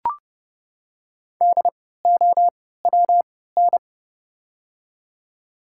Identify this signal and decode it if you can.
Morse code, heard as a keyed tone, about 32 wpm, spaced as for 17 wpm, 700 Hz. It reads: DOWN
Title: 3x Character Spacing - Top 100 Words - 30wpm